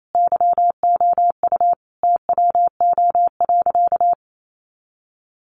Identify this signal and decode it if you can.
Morse code, heard as a keyed tone, about 28 wpm, 700 Hz. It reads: YOU TWO.
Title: Sentences from Top 700 Words 28wpm